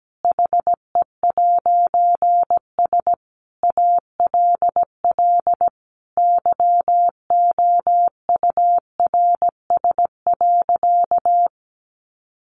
Morse code, heard as a keyed tone, about 17 wpm, 700 Hz. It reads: HE'S ALL YOURS.
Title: A Sound Of Thunder 17wpm